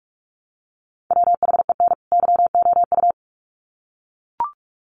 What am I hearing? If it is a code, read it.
Morse code, heard as a keyed tone, about 44 wpm, 700 Hz. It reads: WHEN YOU
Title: Sets of 2 Words 45wpm